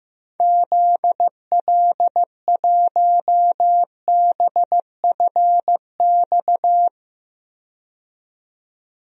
Morse code, heard as a keyed tone, about 15 wpm, 700 Hz. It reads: ZL1BFX
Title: DXCC-100 Call Signs - 6 Characters - 15wpm